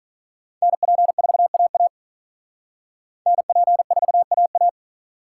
Morse code, heard as a keyed tone, about 43 wpm, 700 Hz. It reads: NP4AA NP4AA
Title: Speed Racing - USA 2x2 Call Signs 35wpm